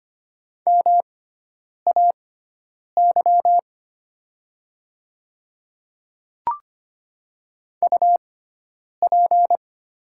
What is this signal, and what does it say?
Morse code, heard as a keyed tone, about 25 wpm, spaced as for 9 wpm, 700 Hz. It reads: MAY UP
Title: 6x Character Spacing - Top 100 Words - 25wpm